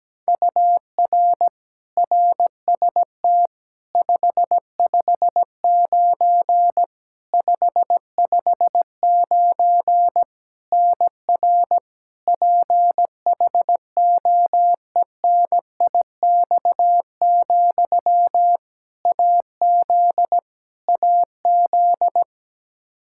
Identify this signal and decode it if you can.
Morse code, heard as a keyed tone, about 17 wpm, 700 Hz. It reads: UR RST 559 559 NR PHOENIX, AZ AZ
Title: Practice QSO - Exchange 1 - Beginning - 17wpm